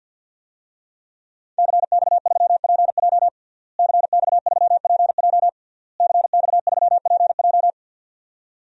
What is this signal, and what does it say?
Morse code, heard as a keyed tone, about 51 wpm, 700 Hz. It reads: CX3PJ CX3PJ CX3PJ
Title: DXCC-100 Call Signs - 5 Characters - Repeated 3x - 50wpm